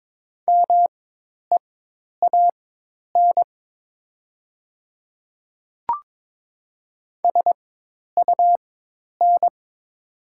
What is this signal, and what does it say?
Morse code, heard as a keyed tone, about 22 wpm, spaced as for 10 wpm, 700 Hz. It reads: MEAN SUN
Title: 4x Character Spacing - Top 300 Words - 22wpm